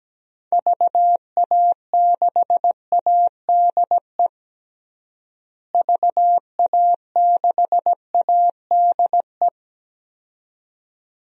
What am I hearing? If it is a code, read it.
Morse code, heard as a keyed tone, about 17 wpm, 700 Hz. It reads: VA6ADE VA6ADE
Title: DXCC-100 Call Signs - 6 Characters - Repeated 2x - 17wpm